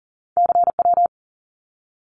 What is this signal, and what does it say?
Morse code, heard as a keyed tone, about 40 wpm, 700 Hz. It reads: CW
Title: Abbreviations Repeated 3x - 40wpm